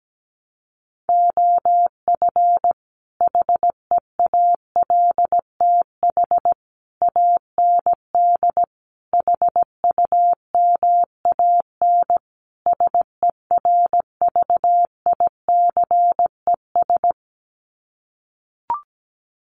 Morse code, heard as a keyed tone, about 17 wpm, 700 Hz. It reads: OF HEALTH AND HUMAN SERVICES
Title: Sets of 5 Words 17wpm